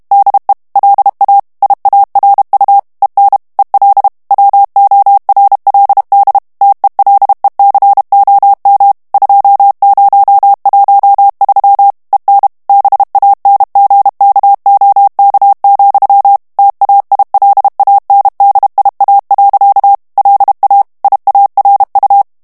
Morse code, heard as a keyed tone, about 32 wpm, 800 Hz. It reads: DE LA IARU EN EL WORLD TELECOM 2013 EN BANGKOK, TAILANDIA. LA IARU
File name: morse mp3 / ESPAGNOL TEXTE EN CLAIR 1 VITESSE 30 MOTS MINUTES